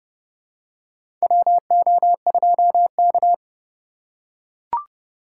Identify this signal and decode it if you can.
Morse code, heard as a keyed tone, about 30 wpm, 700 Hz. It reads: WO2K
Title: USA Call Signs - 4 Characters - 3x Character Spacing - 30wpm